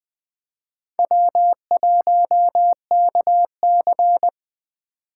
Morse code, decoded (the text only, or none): W1KC